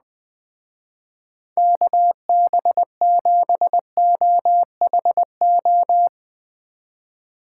Morse code, heard as a keyed tone, about 20 wpm, 700 Hz. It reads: KB7OHO
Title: Speed Racing - USA 2x3 Call Signs 17wpm